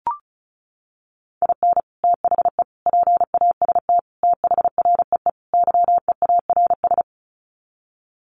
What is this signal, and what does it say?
Morse code, heard as a keyed tone, about 34 wpm, 700 Hz. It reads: IN THE PAST THREE YEARS
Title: Sets of 5 Words Encore2 35wpm